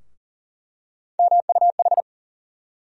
Morse code, heard as a keyed tone, about 40 wpm, 700 Hz. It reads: MUH
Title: USA Call Sign Postfixes - 3 Letters 40wpm